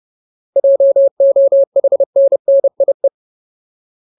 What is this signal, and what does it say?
Morse code, heard as a keyed tone, about 30 wpm, 550 Hz. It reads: JOHNNIE
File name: Ultimate Word List / 30WPM Ultimate